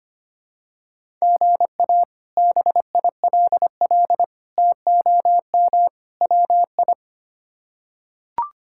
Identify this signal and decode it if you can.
Morse code, heard as a keyed tone, about 25 wpm, 700 Hz. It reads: GA BILL TOM WI
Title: K1USN SST QSO Practice - 25wpm